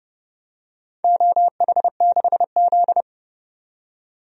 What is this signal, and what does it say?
Morse code, heard as a keyed tone, about 31 wpm, 700 Hz. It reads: OH6Z
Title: DXCC-100 Call Signs - 4 Characters - 30wpm